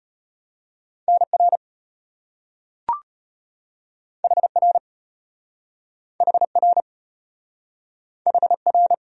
Morse code, heard as a keyed tone, about 33 wpm, 700 Hz. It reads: NR HR HR HR